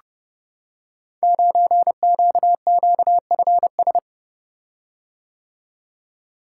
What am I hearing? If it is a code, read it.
Morse code, heard as a keyed tone, about 30 wpm, 700 Hz. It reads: 9QQFS
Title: Code Groups - 5 Letter:Numbers - 30wpm